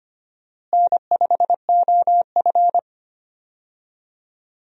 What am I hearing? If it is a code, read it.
Morse code, heard as a keyed tone, about 25 wpm, 700 Hz. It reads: N5OF
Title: USA Call Signs - 4 Characters - Repeated 3x - 25wpm